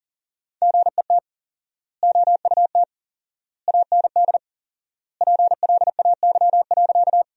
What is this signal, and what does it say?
Morse code, heard as a keyed tone, about 40 wpm, 700 Hz. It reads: GET OUT AND PLAY.
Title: 4x Word-Spacing - Sentences from Top 200 Words - 40wpm